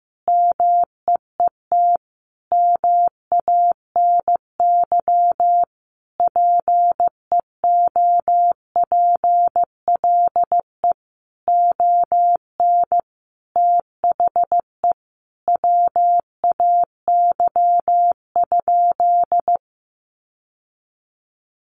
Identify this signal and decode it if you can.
Morse code, heard as a keyed tone, about 14 wpm, 700 Hz. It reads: MEET MANY PEOPLE ON THE WAY?
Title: Sentences from Top 1100 Words 15wpm